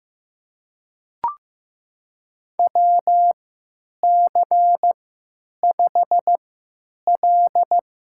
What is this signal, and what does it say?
Morse code, heard as a keyed tone, about 15 wpm, spaced as for 9 wpm, 700 Hz. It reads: WC5L
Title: USA Call Signs - 4 Characters - 3x Character Spacing - 15wpm